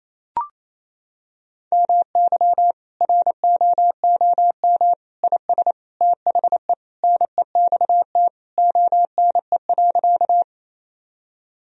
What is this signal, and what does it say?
Morse code, heard as a keyed tone, about 28 wpm, 700 Hz. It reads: MY ROOM IS THE NEXT ONE.
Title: Sentences from Top 1000 Words - Review Part 1 - 28wpm